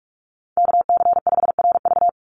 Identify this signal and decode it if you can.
Morse code, heard as a keyed tone, about 44 wpm, 700 Hz. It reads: KC5RV
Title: USA Call Signs 45wpm